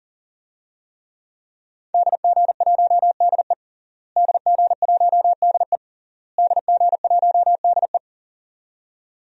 Morse code, heard as a keyed tone, about 40 wpm, 700 Hz. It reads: DG1DE DG1DE DG1DE
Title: DXCC-100 Call Signs - 5 Characters - Repeated 3x - 40wpm